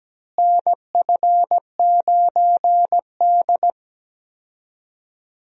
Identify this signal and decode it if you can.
Morse code, heard as a keyed tone, about 17 wpm, 700 Hz. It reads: NF9D
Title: USA Call Signs - 4 Characters - Repeated 2x - 17wpm